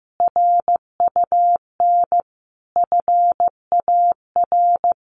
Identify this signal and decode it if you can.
Morse code, heard as a keyed tone, about 15 wpm, 700 Hz. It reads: RUN FAR
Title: Sentences from Top 1100 Words 15wpm